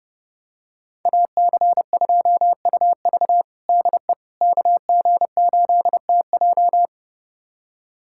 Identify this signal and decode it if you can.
Morse code, heard as a keyed tone, about 30 wpm, 700 Hz. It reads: AC2UV DE KG8TJ
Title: USA Callsign DE Callsign - 5 Characters - 30wpm